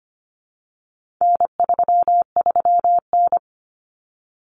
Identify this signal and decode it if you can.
Morse code, heard as a keyed tone, about 25 wpm, 700 Hz. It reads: N33N
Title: Common Cut Numbers - 4-Digit 25wpm